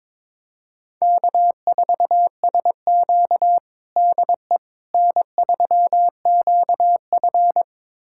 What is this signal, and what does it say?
Morse code, heard as a keyed tone, about 22 wpm, 700 Hz. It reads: K4SQ DE N3QF
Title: USA Callsign DE Callsign - 4 Characters - 22wpm